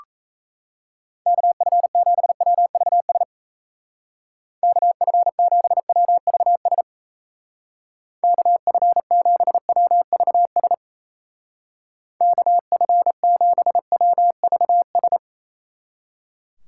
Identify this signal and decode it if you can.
Morse code, heard as a keyed tone, about 34 wpm, 700 Hz. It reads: KF7WVS KF7WVS KF7WVS KF7WVS
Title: Speed Racing - USA 2x3 Call Signs 28wpm